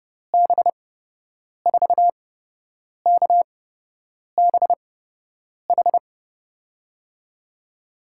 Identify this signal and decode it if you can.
Morse code, heard as a keyed tone, about 30 wpm, spaced as for 8 wpm, 700 Hz. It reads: B4KBH
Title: USA Call Signs - 6 Characters - 8x Character Spacing - 30wpm